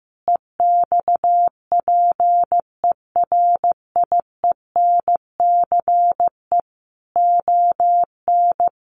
Morse code, heard as a keyed tone, about 15 wpm, 700 Hz. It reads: EXPERIENCE ON
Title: Sentences from Top 1200 Words 15wpm